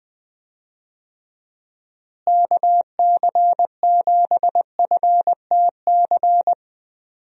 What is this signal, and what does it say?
Morse code, heard as a keyed tone, about 20 wpm, 700 Hz. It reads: KC7FTC
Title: USA Call Signs - 6 Characters - Repeated 3x - 20wpm